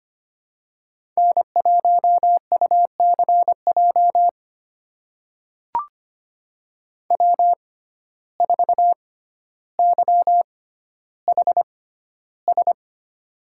Morse code, heard as a keyed tone, about 25 wpm, spaced as for 13 wpm, 700 Hz. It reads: N1UCJ W4YHS
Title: USA Call Signs - 5 Characters - 6x Character Spacing - 25wpm